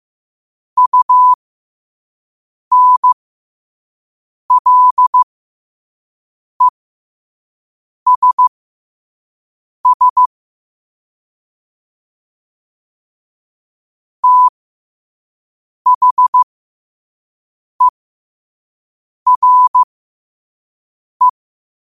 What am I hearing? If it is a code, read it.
Morse code, heard as a keyed tone, about 15 wpm, spaced as for 5 wpm, 1000 Hz. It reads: UNLESS THERE
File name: qotdfiles / 5wpm